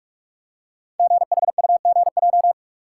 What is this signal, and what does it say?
Morse code, heard as a keyed tone, about 44 wpm, 700 Hz. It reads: GHUGJ
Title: Code Groups - 5 Letters - 45wpm